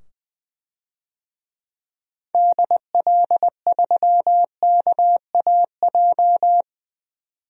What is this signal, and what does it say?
Morse code, heard as a keyed tone, about 20 wpm, 700 Hz. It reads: DL3KAJ
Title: DXCC-100 Call Signs - 6 Characters - 20wpm